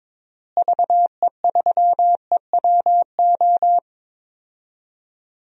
Morse code, heard as a keyed tone, about 22 wpm, 700 Hz. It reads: VE3EWO
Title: DXCC-100 Call Signs - 6 Characters - Repeated 3x - 22wpm